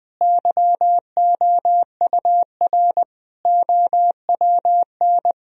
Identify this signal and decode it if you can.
Morse code, heard as a keyed tone, about 20 wpm, 700 Hz. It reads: YOUR OWN
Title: Sentences from Top 1000 Words - Review Part 4 - 20wpm